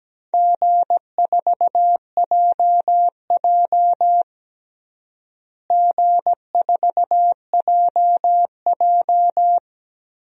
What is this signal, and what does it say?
Morse code, heard as a keyed tone, about 17 wpm, 700 Hz. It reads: G4JJ G4JJ
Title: DXCC-100 Call Signs - 4 Characters - Repeated 3x - 17wpm